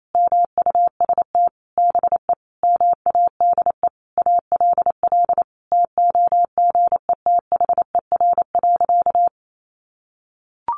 Morse code, heard as a keyed tone, about 28 wpm, 700 Hz. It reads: MUST BE MADE ALL TOGETHER.
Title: Sentences from Top 1100 Words 28wpm